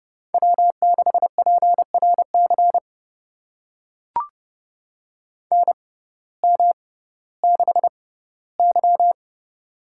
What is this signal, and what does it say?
Morse code, heard as a keyed tone, about 30 wpm, spaced as for 17 wpm, 700 Hz. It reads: W6PRC NM6Y
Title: USA Call Signs - 6x Character Spacing - 30wpm